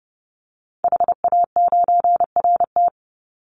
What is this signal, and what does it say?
Morse code, heard as a keyed tone, about 30 wpm, 700 Hz. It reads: HA9RT
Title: Call Signs Repeated 2x - 30wpm